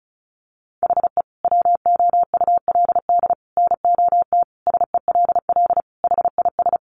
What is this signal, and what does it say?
Morse code, heard as a keyed tone, about 34 wpm, 700 Hz. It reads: HE WOULD NOT SELL HIS
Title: Sentences from Top 1100 Words 35wpm